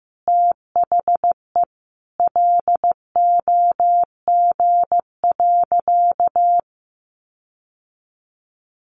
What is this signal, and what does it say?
Morse code, heard as a keyed tone, about 14 wpm, 700 Hz. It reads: THE LOG.
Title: Sentences from Top 1500 Words 15wpm